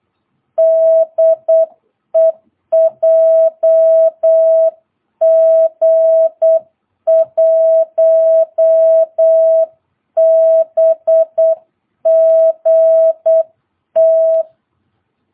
Morse code, heard as a keyed tone, about 8 wpm, 660 Hz. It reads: DE JG1BGT
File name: jh8cbh / 2022cw